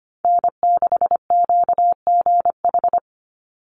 Morse code, heard as a keyed tone, about 25 wpm, 700 Hz. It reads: N6QGH
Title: USA Call Signs 25wpm